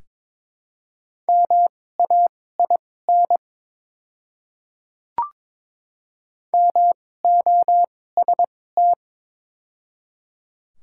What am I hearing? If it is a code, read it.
Morse code, heard as a keyed tone, about 22 wpm, spaced as for 16 wpm, 700 Hz. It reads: MAIN MOST
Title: 2x Character Spacing - Top 300 Words - 22wpm